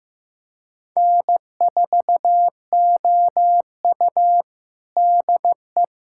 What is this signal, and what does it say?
Morse code, heard as a keyed tone, about 15 wpm, 700 Hz. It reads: N4OU DE
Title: USA Callsign DE Callsign - 4 Characters - 15wpm